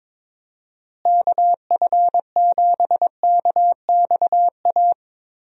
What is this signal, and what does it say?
Morse code, heard as a keyed tone, about 22 wpm, 700 Hz. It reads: KF7KXA
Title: USA Call Signs - 6 Characters - Repeated 3x - 22wpm